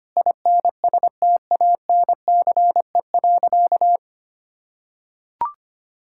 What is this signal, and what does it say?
Morse code, heard as a keyed tone, about 25 wpm, 700 Hz. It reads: INSTANCE.
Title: Sentences from Top 1000 Words - Review Part 5 - 25wpm